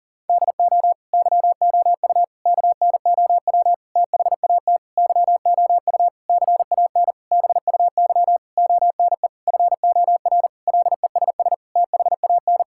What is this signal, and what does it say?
Morse code, heard as a keyed tone, about 40 wpm, 700 Hz. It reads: DO YOU KNOW THAT YOU CAN BUY ONE FOR LESS THAN
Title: Sentences from Top 1000 Words - Review Part 5 - 40wpm